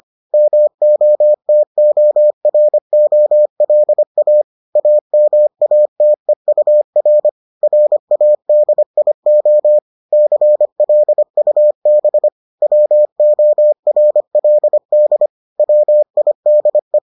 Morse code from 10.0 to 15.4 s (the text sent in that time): CLUB WORLD